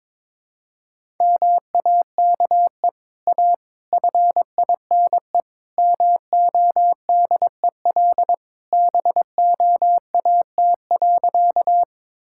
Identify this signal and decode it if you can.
Morse code, heard as a keyed tone, about 21 wpm, 700 Hz. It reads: MAKE A FINE MODEL BOAT.
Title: Sentences from Top 1000 Words - Review Part 5 - 22wpm